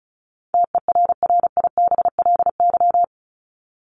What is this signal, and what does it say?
Morse code, heard as a keyed tone, about 34 wpm, 700 Hz. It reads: TERRIBLY
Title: Top 1751-2000 Words 35wpm